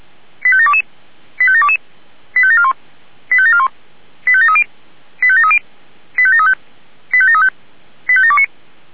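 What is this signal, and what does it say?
Five-tone selective call (ZVEI), heard as quick runs of five tones, tones of about 70 ms. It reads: 85722 85722 85741 85741 85730 85730 85735 85735 85729
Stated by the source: Received via SDR, ZVEI five-tone call.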